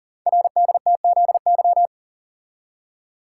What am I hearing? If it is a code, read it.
Morse code, heard as a keyed tone, about 41 wpm, 700 Hz. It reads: RDTZY
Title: Code Groups - 5 Letters - 40wpm